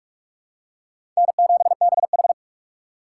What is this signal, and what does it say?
Morse code, heard as a keyed tone, about 45 wpm, 700 Hz. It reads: N7BH